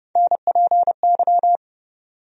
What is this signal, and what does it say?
Morse code, heard as a keyed tone, about 31 wpm, 700 Hz. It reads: NPY
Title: USA Call Sign Postfixes - 3 Letters 30wpm